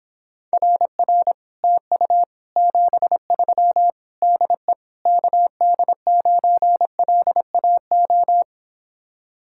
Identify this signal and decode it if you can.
Morse code, heard as a keyed tone, about 29 wpm, 700 Hz. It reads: RR TU 73 DE KD9LAO